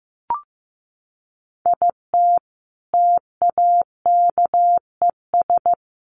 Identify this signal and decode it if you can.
Morse code, heard as a keyed tone, about 15 wpm, 700 Hz. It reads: IT TAKES
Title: Sets of 5 Words Encore 15wpm